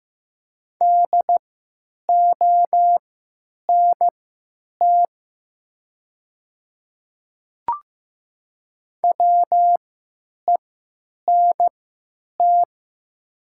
Morse code, heard as a keyed tone, about 15 wpm, spaced as for 9 wpm, 700 Hz. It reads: DONT WENT
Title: 3x Character Spacing - Top 300 Words - 15wpm